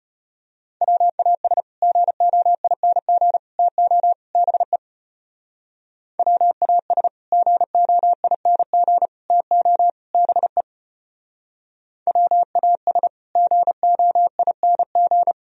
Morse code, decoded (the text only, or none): WAS GOING TO BE WAS GOING TO BE WAS GOING